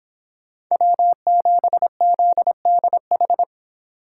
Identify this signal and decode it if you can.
Morse code, heard as a keyed tone, about 28 wpm, 700 Hz. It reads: W7ZDH